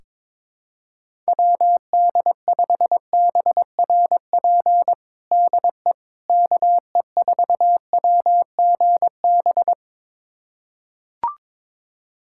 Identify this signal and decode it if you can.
Morse code, heard as a keyed tone, about 22 wpm, 700 Hz. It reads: WD5BRP DE KE4WGB
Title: USA Callsign DE Callsign 22wpm